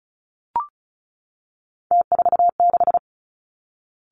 Morse code, heard as a keyed tone, about 36 wpm, 700 Hz. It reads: T46
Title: Three Challenging Letter-Number Combinations 35wpm